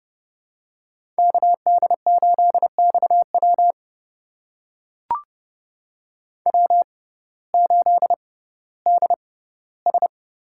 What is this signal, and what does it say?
Morse code, heard as a keyed tone, about 29 wpm, spaced as for 16 wpm, 700 Hz. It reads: KD8XW W8DS
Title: USA Call Signs - 5 Characters - 6x Character Spacing - 30wpm